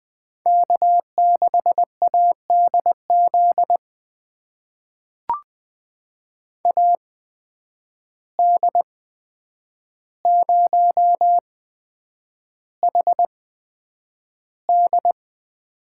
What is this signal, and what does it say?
Morse code, heard as a keyed tone, about 20 wpm, spaced as for 9 wpm, 700 Hz. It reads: K6ADZ AD0HD
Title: USA Call Signs - 5 Characters - 8x Character Spacing - 20wpm